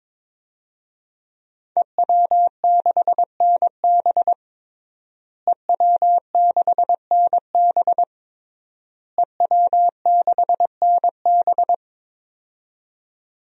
Morse code, heard as a keyed tone, about 22 wpm, 700 Hz. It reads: EW6NB EW6NB EW6NB
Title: DXCC-100 Call Signs - 5 Characters - Repeated 3x - 22wpm